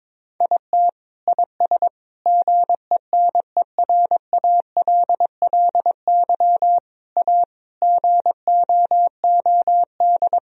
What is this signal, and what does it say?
Morse code, heard as a keyed tone, about 22 wpm, 700 Hz. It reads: IT IS GENERALLY A GOOD
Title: Sentences from Top 1000 Words - Review Part 4 - 22wpm